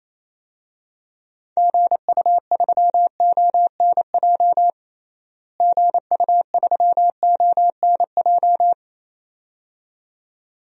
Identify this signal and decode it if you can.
Morse code, heard as a keyed tone, about 27 wpm, 700 Hz. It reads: GU3ONJ GU3ONJ
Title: DXCC-100 Call Signs - 6 Characters - Repeated 2x - 28wpm